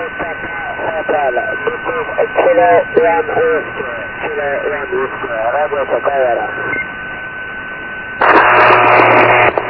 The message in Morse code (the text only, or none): none